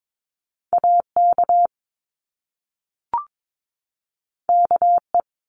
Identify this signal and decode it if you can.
Morse code, heard as a keyed tone, about 22 wpm, 700 Hz. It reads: AK KE
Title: Top Two Letter Combinations in Words 22wpm